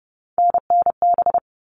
Morse code, heard as a keyed tone, about 30 wpm, 700 Hz. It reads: NNB
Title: All Cut Numbers - 3-Digit 30wpm